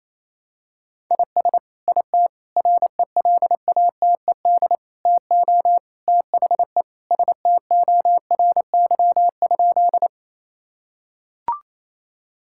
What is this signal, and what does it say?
Morse code, heard as a keyed tone, about 28 wpm, 700 Hz. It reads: IS IT RELATED TO THE STORY?
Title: Sentences from Top 1000 Words - Review Part 5 - 28wpm